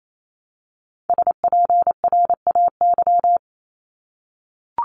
Morse code, heard as a keyed tone, about 28 wpm, 700 Hz. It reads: SPRAY
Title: Top 1751-2000 Words 28wpm